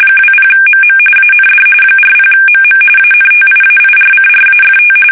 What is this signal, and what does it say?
RTTY, about 45.45 baud. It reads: 20MM AND 40MM ANTIAIRCRAFT
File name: fsk850 / battleshipnj